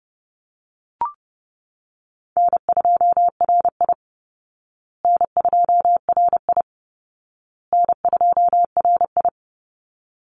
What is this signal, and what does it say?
Morse code, heard as a keyed tone, about 30 wpm, 700 Hz. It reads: N2RI N2RI N2RI
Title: Easy Call Signs Repeated 3x - 30wpm